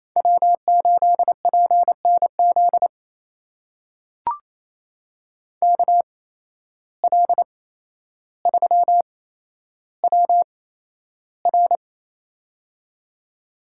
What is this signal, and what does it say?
Morse code, heard as a keyed tone, about 28 wpm, spaced as for 12 wpm, 700 Hz. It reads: W8PNZ KL3WR
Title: USA Call Signs - 5 Characters - 8x Character Spacing - 28wpm